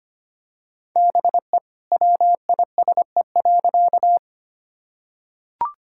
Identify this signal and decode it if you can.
Morse code, heard as a keyed tone, about 25 wpm, 700 Hz. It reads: BE WISE.
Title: Sentences from Top 1600 Words 25wpm